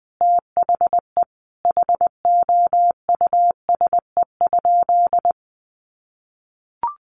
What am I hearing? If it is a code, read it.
Morse code, heard as a keyed tone, about 20 wpm, 700 Hz. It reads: THE HOUSE?
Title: Sentences from Top 200 Words 20wpm